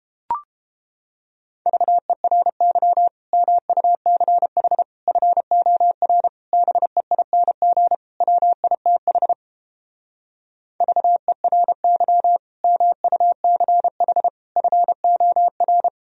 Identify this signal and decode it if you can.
Morse code, heard as a keyed tone, about 31 wpm, 700 Hz. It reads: VERY MUCH FOR BEING WITH VERY MUCH FOR